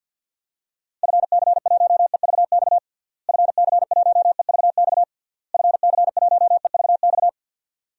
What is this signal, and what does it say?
Morse code, heard as a keyed tone, about 48 wpm, 700 Hz. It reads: FC1EVX FC1EVX FC1EVX
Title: DXCC-100 Call Signs - 6 Characters - Repeated 3x - 50wpm